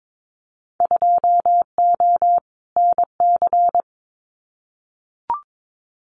Morse code, heard as a keyed tone, about 22 wpm, 700 Hz. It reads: 2O NC
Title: Winter Field Day Exchanges 22wpm